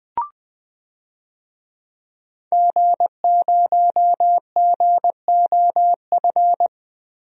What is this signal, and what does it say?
Morse code, heard as a keyed tone, about 20 wpm, 700 Hz. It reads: G0GOF
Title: DXCC-100 Call Signs - 5 Characters - Repeated 2x - 20wpm